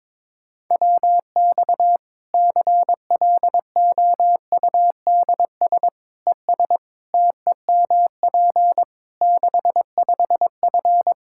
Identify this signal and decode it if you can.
Morse code, heard as a keyed tone, about 22 wpm, 700 Hz. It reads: WX CLOUDS ES TEMP 65F
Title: Practice QSO - Exchange 2 - Info - 22wpm